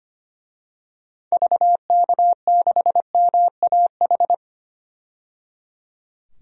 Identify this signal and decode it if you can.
Morse code, heard as a keyed tone, about 25 wpm, 700 Hz. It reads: VK6MAH
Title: DXCC-100 Call Signs - 6 Characters - 25wpm